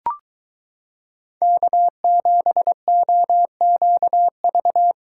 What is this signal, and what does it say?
Morse code, heard as a keyed tone, about 22 wpm, 700 Hz. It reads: K7OQV